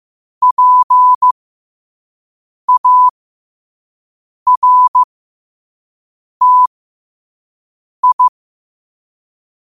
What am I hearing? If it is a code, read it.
Morse code, heard as a keyed tone, about 15 wpm, spaced as for 5 wpm, 1000 Hz. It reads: PARTI
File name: qotdfiles / 5wpm